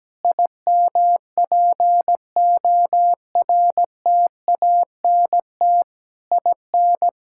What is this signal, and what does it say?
Morse code, heard as a keyed tone, about 17 wpm, 700 Hz. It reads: IMPORTANT IN
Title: Sentences from Top 1000 Words - Review Part 6 - 17wpm